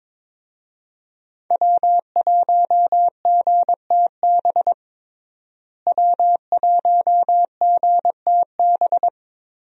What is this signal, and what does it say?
Morse code, heard as a keyed tone, about 22 wpm, 700 Hz. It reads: W1GTB W1GTB
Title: USA Call Signs - 5 Characters - Repeated 2x - 22wpm